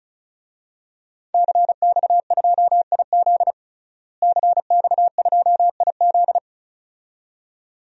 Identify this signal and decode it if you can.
Morse code, heard as a keyed tone, about 36 wpm, 700 Hz. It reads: CX2IZ CX2IZ
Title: DXCC-100 Call Signs - 5 Characters - Repeated 2x - 35wpm